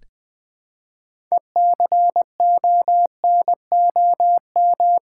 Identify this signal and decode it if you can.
Morse code, heard as a keyed tone, about 20 wpm, 700 Hz. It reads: ECONOM
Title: Build A Word - 2 Letters At A Time - 8-Letter Words - 20wpm